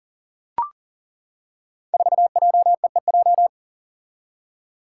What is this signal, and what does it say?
Morse code, heard as a keyed tone, about 42 wpm, 700 Hz. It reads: 4JEEJ
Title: Five Challenging Letter-Number Combinations 40wpm